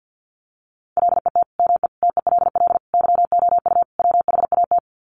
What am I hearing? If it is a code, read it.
Morse code, heard as a keyed tone, about 50 wpm, 700 Hz. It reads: LET ME TELL YOU WHAT
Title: Sets of 5 Words Encore 50wpm